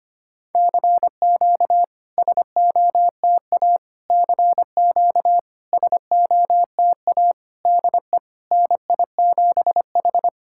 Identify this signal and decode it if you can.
Morse code, heard as a keyed tone, about 25 wpm, 700 Hz. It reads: CQ SOTA CQ SOTA DE NI7H
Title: SOTA QSO Practice 25wpm